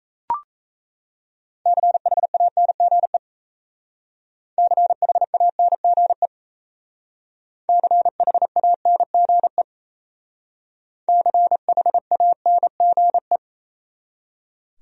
Morse code, heard as a keyed tone, about 35 wpm, 700 Hz. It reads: CHANGE CHANGE CHANGE CHANGE